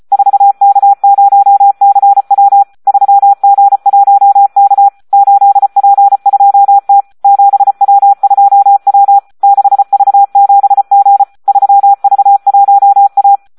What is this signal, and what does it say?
Morse code, heard as a keyed tone, about 34 wpm, 800 Hz. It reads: 4K0CW 3G1K 8P2T 7W2W 6V7G 3V1A
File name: Audio practica CW / SD 1 34